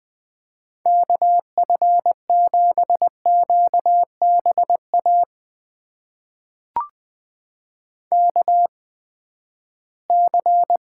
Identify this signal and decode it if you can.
Morse code, heard as a keyed tone, about 20 wpm, spaced as for 14 wpm, 700 Hz. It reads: KF7QBA KC
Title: USA Call Signs - 6 Characters - 8x Character Spacing - 20wpm